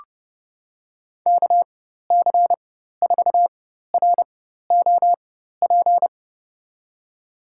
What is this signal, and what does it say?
Morse code, heard as a keyed tone, about 30 wpm, spaced as for 14 wpm, 700 Hz. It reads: KC4ROP
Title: USA Call Signs - 4x Character Spacing - 30wpm